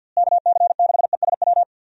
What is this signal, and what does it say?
Morse code, heard as a keyed tone, about 51 wpm, 700 Hz. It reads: KC6ESW
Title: USA Call Signs - 6 Characters - Repeated 2x - 50wpm